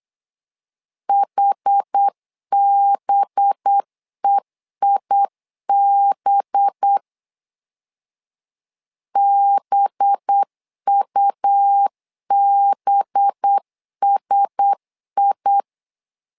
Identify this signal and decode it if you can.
Morse code, heard as a keyed tone, about 8 wpm, 800 Hz. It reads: HBEIB BUBSI
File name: LEZIONI / 5 LBVC / 5 LB 8wpm 2816